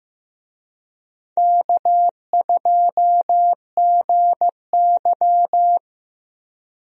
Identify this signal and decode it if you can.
Morse code, heard as a keyed tone, about 15 wpm, 700 Hz. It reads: K2GY
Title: USA Call Signs - 4 Characters - Repeated 2x - 15wpm